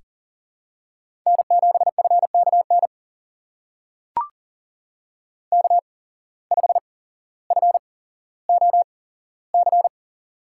Morse code, heard as a keyed tone, about 40 wpm, spaced as for 17 wpm, 700 Hz. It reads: N7FKN K5FOC
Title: USA Call Signs - 5 Characters - 8x Character Spacing - 40wpm